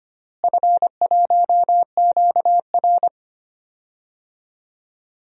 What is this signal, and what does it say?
Morse code, heard as a keyed tone, about 25 wpm, 700 Hz. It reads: F1QR
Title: DXCC-100 Call Signs - 4 Characters - Repeated 3x - 25wpm